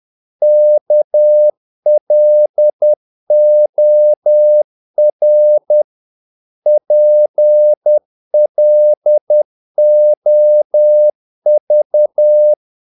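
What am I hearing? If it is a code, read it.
Morse code, heard as a keyed tone, about 10 wpm, 600 Hz. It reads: KLOR PLOV
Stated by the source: Morse practice of short Danish words at 10 wpm, from CW med Gnister.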